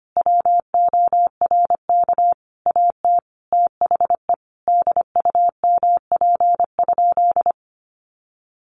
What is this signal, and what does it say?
Morse code, heard as a keyed tone, about 25 wpm, 700 Hz. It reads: WORK AT THE DUMP?
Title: Sentences from Top 1300 Words 25wpm